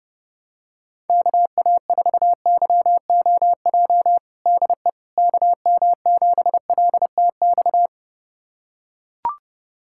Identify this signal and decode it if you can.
Morse code, heard as a keyed tone, about 30 wpm, 700 Hz. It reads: KA4YOJ DE KM7LTX
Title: USA Callsign DE Callsign 30wpm